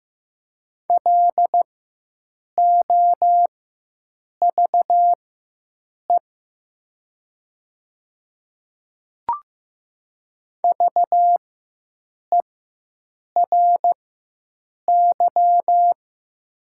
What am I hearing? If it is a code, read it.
Morse code, heard as a keyed tone, about 15 wpm, spaced as for 7 wpm, 700 Hz. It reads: LOVE VERY
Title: 4x Character Spacing - Top 500 Words - 15wpm